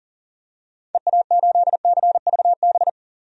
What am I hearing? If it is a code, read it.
Morse code, heard as a keyed tone, about 40 wpm, 700 Hz. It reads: EA8CVB
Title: DXCC-100 Call Signs - 6 Characters - 40wpm